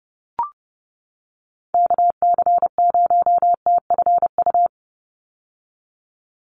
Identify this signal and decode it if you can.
Morse code, heard as a keyed tone, about 30 wpm, 700 Hz. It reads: KC0TFU
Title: USA Call Signs 30wpm